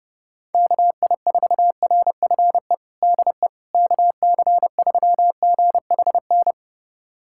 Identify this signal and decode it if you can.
Morse code, heard as a keyed tone, about 32 wpm, 700 Hz. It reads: KI4RFE DE KC3GHN
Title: USA Callsign DE Callsign - 6 Characters - 30wpm